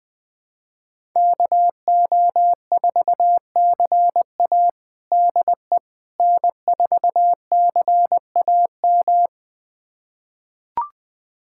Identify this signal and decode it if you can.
Morse code, heard as a keyed tone, about 20 wpm, 700 Hz. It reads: KO4CA DE N4CAM
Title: USA Callsign DE Callsign - 5 Characters - 20wpm